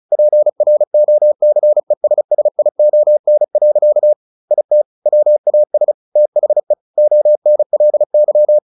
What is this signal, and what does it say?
Morse code, heard as a keyed tone, about 34 wpm, 600 Hz. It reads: PROCESSION. IT WAS THE ONLY
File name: the old man and the sea 2 35wpm